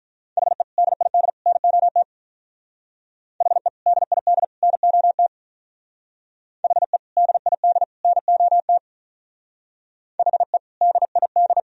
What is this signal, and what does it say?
Morse code, heard as a keyed tone, about 46 wpm, 700 Hz. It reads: HE DID NOT HE DID NOT HE DID NOT HE DID